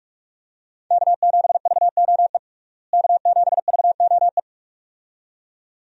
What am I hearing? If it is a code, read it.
Morse code, heard as a keyed tone, about 46 wpm, 700 Hz. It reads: K7VOE K7VOE
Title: USA Call Signs - 5 Characters - Repeated 2x - 45wpm